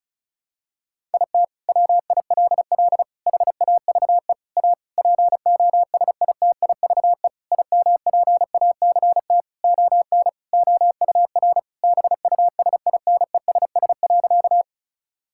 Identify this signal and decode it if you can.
Morse code, heard as a keyed tone, about 36 wpm, 700 Hz. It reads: IT WILL HAVE A POSITIVE IMPACT ON OUR BUSINESS.
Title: Sentences from Top 1000 Words - Review Part 6 - 35wpm